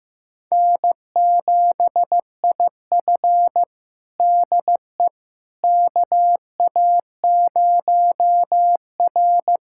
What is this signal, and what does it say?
Morse code, heard as a keyed tone, about 15 wpm, 700 Hz. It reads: N7IF DE KA0R
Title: USA Callsign DE Callsign - 4 Characters - 15wpm